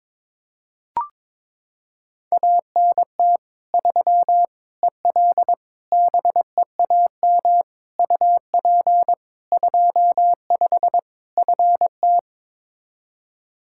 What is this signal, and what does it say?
Morse code, heard as a keyed tone, about 22 wpm, 700 Hz. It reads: ANT 3 EL BEAM UP 25 FT
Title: Practice QSO - Exchange 2 - Info - 22wpm